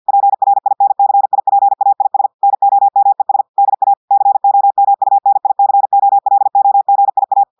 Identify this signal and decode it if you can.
Morse code, heard as a keyed tone, about 50 wpm, 800 Hz. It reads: PRINCIPAIS NOMES DA COMUNICOLOGIA
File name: audios / portugais / portugais texte en clair 1 vitesse 50 mots minutes